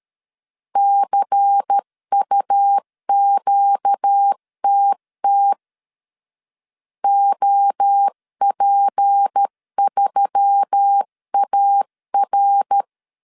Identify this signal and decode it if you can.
Morse code, heard as a keyed tone, about 13 wpm, 800 Hz. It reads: CUQTT OP3AR
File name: CW / DET / 1 6 1500C S C 12wpm 3533